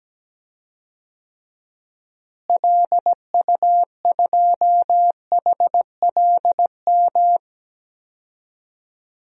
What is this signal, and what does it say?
Morse code, heard as a keyed tone, about 17 wpm, 700 Hz. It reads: LU2HLM
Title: DXCC-100 Call Signs - 6 Characters - Repeated 3x - 17wpm